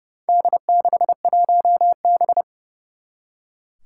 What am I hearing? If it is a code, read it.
Morse code, heard as a keyed tone, about 30 wpm, 700 Hz. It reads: D61B
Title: Four Challenging Letter-Number Combinations 30wpm